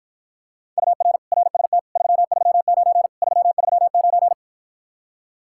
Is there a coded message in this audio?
Yes: Morse code from 0.8 to 4.3 s, heard as a keyed tone, about 54 wpm, 700 Hz. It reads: UR RST 339 339